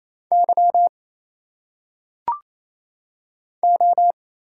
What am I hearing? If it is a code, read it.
Morse code, heard as a keyed tone, about 28 wpm, 700 Hz. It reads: Y O
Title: Single Letters - Phonetic Pronunciation 28wpm